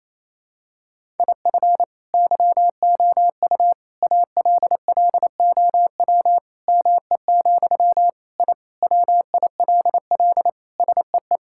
Morse code, heard as a keyed tone, about 28 wpm, 700 Hz. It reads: IF YOU ALLOW ME, I WILL SEE
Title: Sentences from Top 1000 Words - Review Part 5 - 28wpm